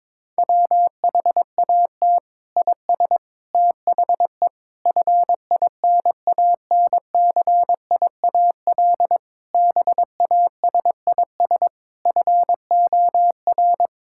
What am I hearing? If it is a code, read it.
Morse code, heard as a keyed tone, about 22 wpm, 700 Hz. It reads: WHAT IS THE FINANCIAL BASIS FOR
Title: Sentences from Top 1000 Words - Review Part 5 - 22wpm